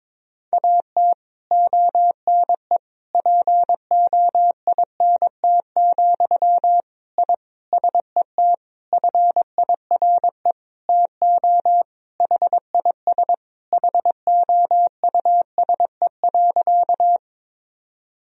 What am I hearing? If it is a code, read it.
Morse code, heard as a keyed tone, about 22 wpm, 700 Hz. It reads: AT ONE POINT, I SET FIRE TO HIS HOUSE.
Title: Sentences from Top 1000 Words - Review Part 1 - 22wpm